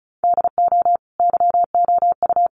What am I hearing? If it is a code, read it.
Morse code, heard as a keyed tone, about 34 wpm, 700 Hz. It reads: DO YOU
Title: Sentences from Top 1200 Words 35wpm